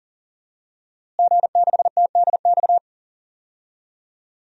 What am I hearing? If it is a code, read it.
Morse code, heard as a keyed tone, about 40 wpm, 700 Hz. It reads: G6TDX
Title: DXCC-100 Call Signs - 5 Characters - Repeated 2x - 40wpm